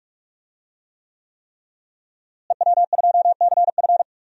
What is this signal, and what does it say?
Morse code, heard as a keyed tone, about 44 wpm, 700 Hz. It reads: EW2CF
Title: DXCC-100 Call Signs - 5 Characters - Repeated 3x - 45wpm